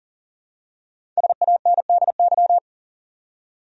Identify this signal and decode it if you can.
Morse code, heard as a keyed tone, about 40 wpm, 700 Hz. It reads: SANDY
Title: USA Common Names - 40wpm